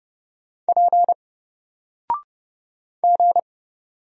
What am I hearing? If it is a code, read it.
Morse code, heard as a keyed tone, about 29 wpm, 700 Hz. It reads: P G
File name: Single-Letter-Number-30wpm